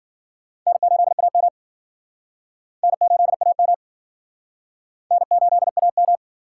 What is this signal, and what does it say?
Morse code, heard as a keyed tone, about 56 wpm, 700 Hz. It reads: N8AM N8AM N8AM